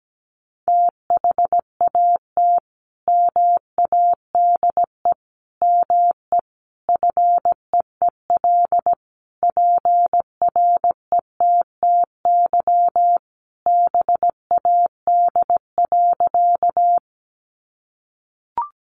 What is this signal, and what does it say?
Morse code, heard as a keyed tone, about 17 wpm, 700 Hz. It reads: THAT MADE ME FEEL PRETTY BAD.
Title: Sentences from Top 1000 Words 17wpm